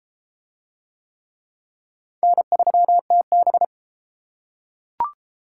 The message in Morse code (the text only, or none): N3TB